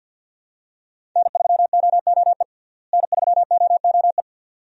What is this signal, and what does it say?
Morse code, heard as a keyed tone, about 52 wpm, 700 Hz. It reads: N3OOE N3OOE
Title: USA Call Signs - 5 Characters - Repeated 2x - 50wpm